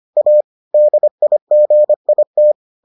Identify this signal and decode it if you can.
Morse code, heard as a keyed tone, about 25 wpm, 600 Hz. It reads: A DIGIT